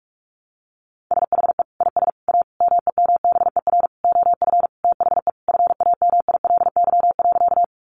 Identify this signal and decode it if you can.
Morse code, heard as a keyed tone, about 44 wpm, 700 Hz. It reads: SHE IS A MEMBER OF THE FAMILY.
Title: Sentences from Top 1100 Words 45wpm